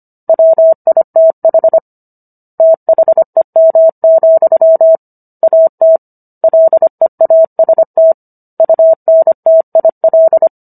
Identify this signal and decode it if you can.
Morse code, heard as a keyed tone, about 25 wpm, 650 Hz. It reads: WITH THEM, AT LEAST UNTIL